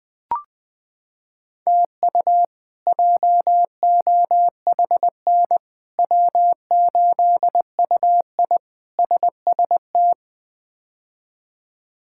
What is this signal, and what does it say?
Morse code, heard as a keyed tone, about 21 wpm, 700 Hz. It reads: TU JOHN W8UI SST
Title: K1USN SST QSO Practice - 20wpm